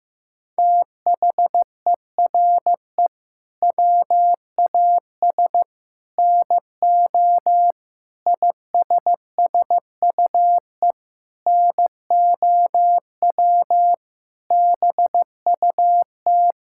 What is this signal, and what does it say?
Morse code, heard as a keyed tone, about 14 wpm, 700 Hz. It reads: THERE WAS NO ISSUE NOW BUT
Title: Sentences from Top 1000 Words - Review Part 6 - 15wpm